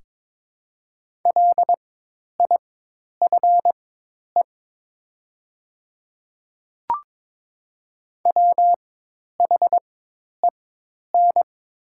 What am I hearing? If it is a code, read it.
Morse code, heard as a keyed tone, about 21 wpm, spaced as for 10 wpm, 700 Hz. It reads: LIFE WHEN
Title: 4x Character Spacing - Top 500 Words - 22wpm